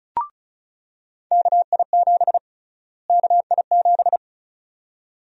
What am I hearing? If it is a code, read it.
Morse code, heard as a keyed tone, about 34 wpm, 700 Hz. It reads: KI7 KI7
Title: USA Call Sign Prefixes - Repeated 2x - 35wpm